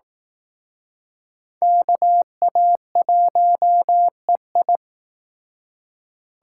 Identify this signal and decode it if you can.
Morse code, heard as a keyed tone, about 18 wpm, 700 Hz. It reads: KA1EI
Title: Speed Racing - USA 2x2 Call Signs 15wpm